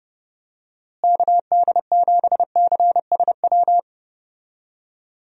Morse code, heard as a keyed tone, about 30 wpm, 700 Hz. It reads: KD7CSW
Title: USA Call Signs - 6 Characters - Repeated 3x - 30wpm